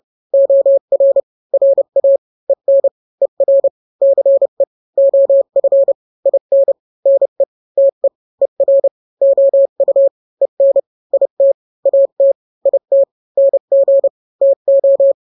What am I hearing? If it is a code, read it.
Morse code, heard as a keyed tone, about 30 wpm, 550 Hz. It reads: OR RA EN ER CE OF IN NE TE ER OU EN IT AT IT NG TO
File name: n-grams / 2-grams 30 25 WPM